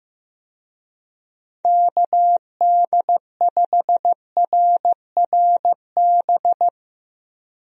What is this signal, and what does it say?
Morse code, heard as a keyed tone, about 16 wpm, 700 Hz. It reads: KD5RRB
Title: USA Call Signs - 6 Characters - Repeated 2x - 15wpm